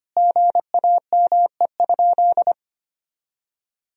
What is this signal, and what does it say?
Morse code, heard as a keyed tone, about 25 wpm, 700 Hz. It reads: GAME?
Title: Sentences from Top 1000 Words - Review Part 4 - 25wpm